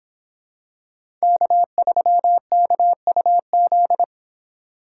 Morse code, heard as a keyed tone, about 25 wpm, 700 Hz. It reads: K3KUZ